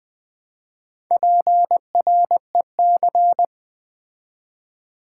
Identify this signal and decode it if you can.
Morse code, heard as a keyed tone, about 20 wpm, 700 Hz. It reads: PREC
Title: Build A Word - 2 Letters At A Time - 8-Letter Words - 20wpm